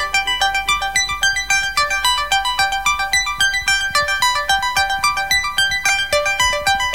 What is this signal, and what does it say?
Emergency Alert System (SAME) tones (header bursts), heard as harsh buzzing bursts. It is heard from 0.9 to 6.0 s.
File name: mp3files / off the hook -128